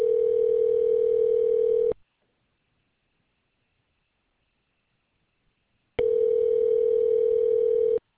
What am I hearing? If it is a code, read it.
Ringback tone (North America), 440 + 480 Hz.